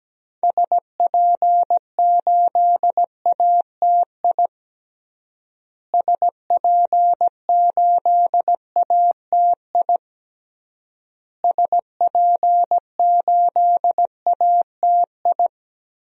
Morse code, heard as a keyed tone, about 17 wpm, 700 Hz. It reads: SP8ATI SP8ATI SP8ATI
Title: DXCC-100 Call Signs - 6 Characters - Repeated 3x - 17wpm